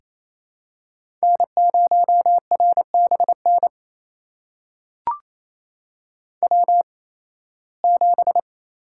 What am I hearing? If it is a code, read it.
Morse code, heard as a keyed tone, about 28 wpm, spaced as for 18 wpm, 700 Hz. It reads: N0RBN W7
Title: USA Call Signs - 5 Characters - 8x Character Spacing - 28wpm